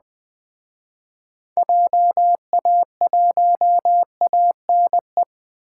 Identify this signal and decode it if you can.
Morse code, heard as a keyed tone, about 20 wpm, 700 Hz. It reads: JA1ANE
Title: Speed Racing - USA 2x3 Call Signs 17wpm